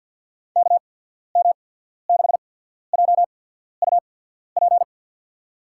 Morse code, heard as a keyed tone, about 53 wpm, spaced as for 14 wpm, 700 Hz. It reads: KM6JUP
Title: USA Call Signs - 6 Characters - 8x Character Spacing - 50wpm